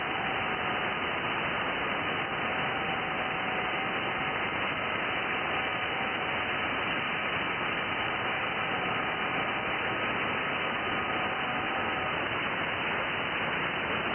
FT8.
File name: IC-7100